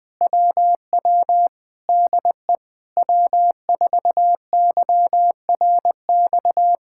Morse code, heard as a keyed tone, about 20 wpm, 700 Hz. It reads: WW DE W4YRX
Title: Speed-Racing Common QSO Phrases - 17wpm